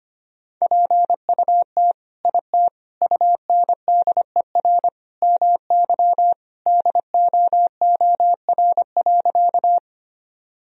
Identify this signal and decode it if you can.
Morse code, heard as a keyed tone, about 25 wpm, 700 Hz. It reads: PUT IT UNDER MY DOOR.
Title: Sentences from Top 1000 Words - Review Part 1 - 25wpm